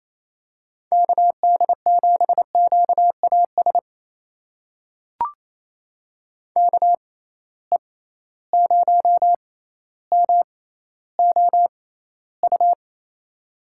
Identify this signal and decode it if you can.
Morse code, heard as a keyed tone, about 28 wpm, spaced as for 14 wpm, 700 Hz. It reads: KD7QAS KE0MOU
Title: USA Call Signs - 6 Characters - 6x Character Spacing - 28wpm